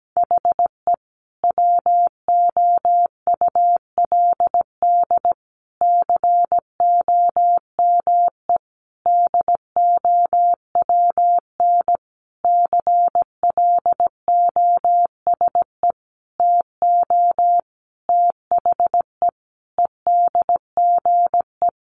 Morse code, heard as a keyed tone, about 17 wpm, 700 Hz. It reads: HE WOULD COME DOWN CLOSE TO THE EDGE
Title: Sentences from Top 1300 Words 17wpm